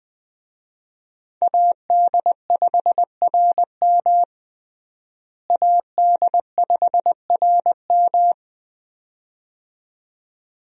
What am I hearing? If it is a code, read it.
Morse code, heard as a keyed tone, about 20 wpm, 700 Hz. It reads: AD5RM AD5RM
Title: USA Call Signs - 5 Characters - Repeated 2x - 20wpm